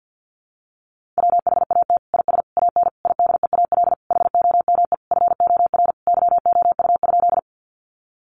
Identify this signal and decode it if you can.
Morse code, heard as a keyed tone, about 50 wpm, 700 Hz. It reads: WHAT IS AN IDEAL HOME FOR YOU?
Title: Sentences from Top 1400 Words 50wpm